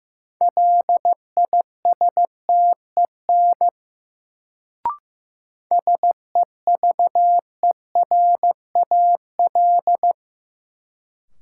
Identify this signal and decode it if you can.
Morse code, heard as a keyed tone, about 15 wpm, 700 Hz. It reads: LISTEN SEVERAL
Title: Sentences from Top 400 Words - New Word Drill - 15wpm